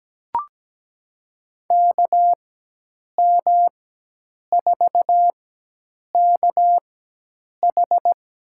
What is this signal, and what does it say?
Morse code, heard as a keyed tone, about 17 wpm, spaced as for 8 wpm, 700 Hz. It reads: KM4KH
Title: USA Call Signs - 4x Character Spacing - 17wpm